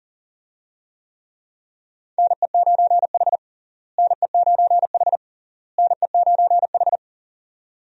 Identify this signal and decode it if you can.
Morse code, heard as a keyed tone, about 40 wpm, 700 Hz. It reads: NE9H NE9H NE9H
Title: USA Call Signs - 4 Characters - Repeated 3x - 40wpm